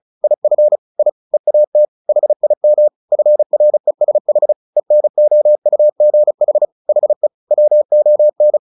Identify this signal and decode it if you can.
Morse code, heard as a keyed tone, about 34 wpm, 600 Hz. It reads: IF I EAT HIM FRESH ENOUGH HE WON
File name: the old man and the sea / the old man and the sea 1 35wpm